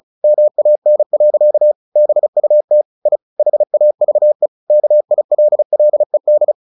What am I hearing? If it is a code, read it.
Morse code, heard as a keyed tone, about 34 wpm, 600 Hz. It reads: MAN. BUT I HAVE KILLED
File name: the old man and the sea 2 35wpm